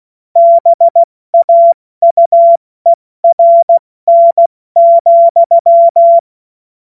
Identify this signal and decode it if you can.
Morse code, heard as a keyed tone, about 16 wpm, spaced as for 14 wpm, 680 Hz. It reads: BAUERN,
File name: Morsekurs / Witze / Witz 24 Juni Mai 2025 70 F80